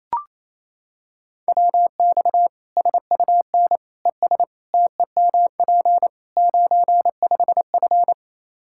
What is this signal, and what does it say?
Morse code, heard as a keyed tone, about 28 wpm, 700 Hz. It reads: WX SUN ES TEMP 95F
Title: K1USN SST QSO Practice 28wpm